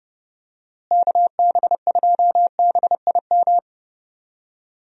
Morse code, heard as a keyed tone, about 30 wpm, 700 Hz. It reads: KB2BIM
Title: USA Call Signs - 6 Characters - Repeated 2x - 30wpm